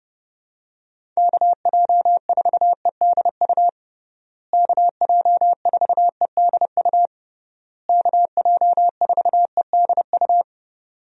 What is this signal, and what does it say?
Morse code, heard as a keyed tone, about 31 wpm, 700 Hz. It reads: KJ4EDU KJ4EDU KJ4EDU
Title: USA Call Signs - 6 Characters - Repeated 3x - 30wpm